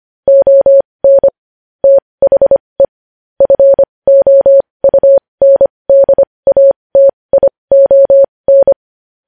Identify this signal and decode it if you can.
Morse code, heard as a keyed tone, about 25 wpm, spaced as for 20 wpm, 550 Hz. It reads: ON THE FOUNDATION
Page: news headlines in morse code at 20 wpm